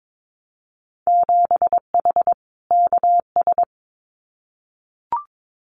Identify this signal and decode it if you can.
Morse code, heard as a keyed tone, about 22 wpm, 700 Hz. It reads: 7H KS
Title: Winter Field Day Exchanges Encore 22wpm